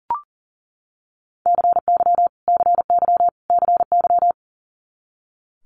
Morse code, heard as a keyed tone, about 40 wpm, 700 Hz. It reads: CY CY CY
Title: Multi-letter Phonemes Repeated 3x 40wpm